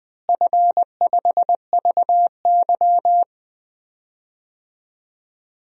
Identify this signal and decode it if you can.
Morse code, heard as a keyed tone, about 20 wpm, 700 Hz. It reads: F5VY
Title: DXCC-100 Call Signs - 4 Characters - Repeated 3x - 20wpm